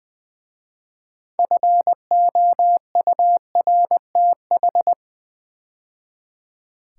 Morse code, heard as a keyed tone, about 20 wpm, 700 Hz. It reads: FOURTH
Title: 6 Letter Words - First 3 Letters Given - 20wpm